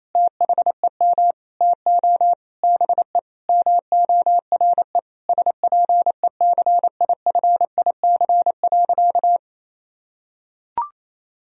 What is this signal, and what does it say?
Morse code, heard as a keyed tone, about 28 wpm, 700 Hz. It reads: THEM TO BE MORE SPECIFIC.
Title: Sentences from Top 1000 Words - Review Part 3 - 28wpm